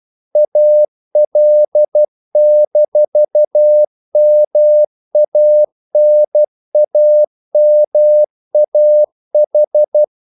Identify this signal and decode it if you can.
Morse code, heard as a keyed tone, about 12 wpm, 600 Hz. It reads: AL-MANAMAH
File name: CW / Practice / NationalCapitals12